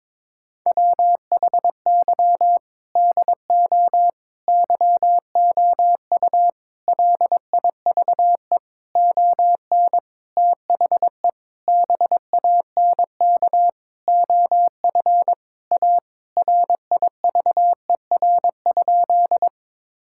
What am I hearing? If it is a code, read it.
Morse code, heard as a keyed tone, about 22 wpm, 700 Hz. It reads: WHY DO YOU LIVE ON THE BANK OF A RIVER?
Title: Sentences from Top 1000 Words - Review Part 6 - 22wpm